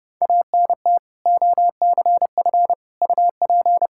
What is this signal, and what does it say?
Morse code, heard as a keyed tone, about 30 wpm, 700 Hz. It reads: ANT OCF UP
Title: Practice QSO - Exchange 2 - Info - 30wpm